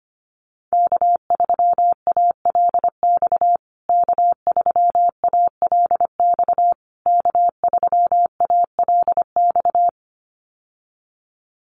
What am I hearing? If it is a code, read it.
Morse code, heard as a keyed tone, about 25 wpm, 700 Hz. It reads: K3ALX K3ALX K3ALX
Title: USA Call Signs Repeated 3x 25wpm